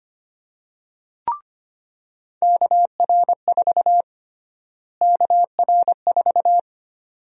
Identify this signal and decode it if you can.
Morse code, heard as a keyed tone, about 25 wpm, 700 Hz. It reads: KR4 KR4
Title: USA Call Sign Prefixes - Repeated 2x - 25wpm